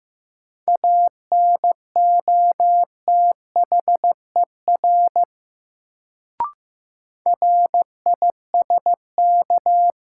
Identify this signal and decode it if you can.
Morse code, heard as a keyed tone, about 15 wpm, 700 Hz. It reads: ANOTHER RISK
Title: Sentences from Top 600 Words - New Word Drill - 15wpm